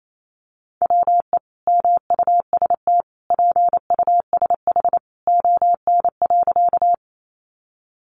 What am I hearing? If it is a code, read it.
Morse code, heard as a keyed tone, about 28 wpm, 700 Hz. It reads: WE MUST PUSH ON.
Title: Sentences from Top 1300 Words 28wpm